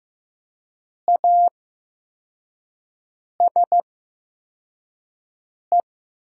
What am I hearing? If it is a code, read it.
Morse code, heard as a keyed tone, about 15 wpm, spaced as for 4 wpm, 700 Hz. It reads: ASE
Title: 8x Character Spacing - Top 300 Words - 15wpm